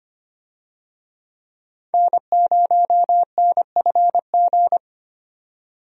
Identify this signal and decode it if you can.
Morse code, heard as a keyed tone, about 25 wpm, 700 Hz. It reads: N0NFG
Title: USA Call Signs - 5 Characters - 25wpm